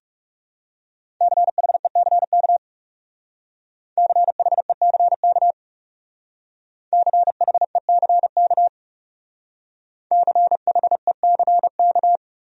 Morse code, heard as a keyed tone, about 37 wpm, 700 Hz. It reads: CHECK CHECK CHECK CHECK
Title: Speed-Racing Top 500 Words 30wpm